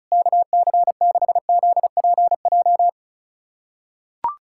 Morse code, heard as a keyed tone, about 35 wpm, 700 Hz. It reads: KC6ZPJ
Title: USA Callsign DE Callsign - 6 Characters - 35wpm